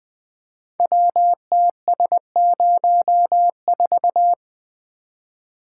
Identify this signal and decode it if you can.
Morse code, heard as a keyed tone, about 20 wpm, 700 Hz. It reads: WTS04
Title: Code Groups - 5 Letter:Numbers - 20wpm